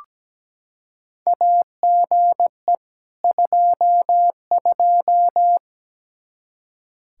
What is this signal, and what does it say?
Morse code, heard as a keyed tone, about 17 wpm, 700 Hz. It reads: AGE 22
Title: Practice QSO - Exchange 3 - More Info - 17wpm